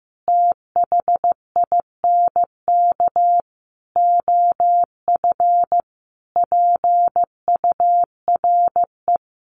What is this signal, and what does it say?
Morse code, heard as a keyed tone, about 15 wpm, 700 Hz. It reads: THINK OF PURE
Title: Sentences from Top 1500 Words 15wpm